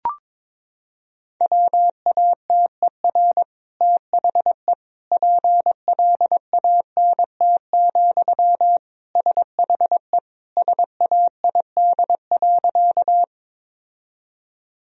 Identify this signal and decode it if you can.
Morse code, heard as a keyed tone, about 22 wpm, 700 Hz. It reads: WATER THE PLANT, SHE SAID.
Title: Sentences from Top 1000 Words - Review Part 1 - 22wpm